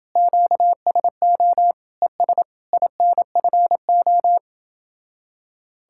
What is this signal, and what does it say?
Morse code, heard as a keyed tone, about 29 wpm, 700 Hz. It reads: QSO ES INFO